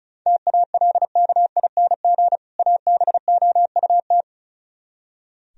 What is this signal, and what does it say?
Morse code, heard as a keyed tone, about 42 wpm, 700 Hz. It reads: TALKING ABOUT
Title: Speed-Racing - Sets of 2 Words 35wpm